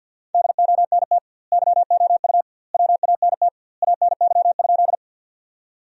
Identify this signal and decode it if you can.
Morse code, heard as a keyed tone, about 47 wpm, 700 Hz. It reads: DONT YOU WANT ANY?
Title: 2x Word-Spacing - Sentences from Top 300 Words - 50wpm